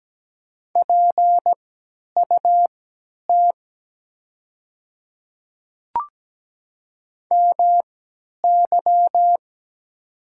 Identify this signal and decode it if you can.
Morse code, heard as a keyed tone, about 17 wpm, spaced as for 10 wpm, 700 Hz. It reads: PUT MY
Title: 3x Character Spacing - Top 200 Words - 17wpm